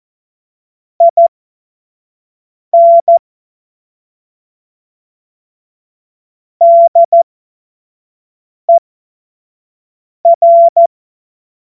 Morse code, heard as a keyed tone, about 14 wpm, spaced as for 5 wpm, 680 Hz. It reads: IN DER
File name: Morsekurs / Witze / Witz 25. Oktober 2022 25 F70